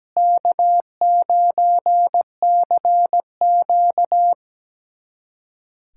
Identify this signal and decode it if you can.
Morse code, heard as a keyed tone, about 17 wpm, 700 Hz. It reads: K9CQ
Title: USA Callsign DE Callsign - 4 Characters - 17wpm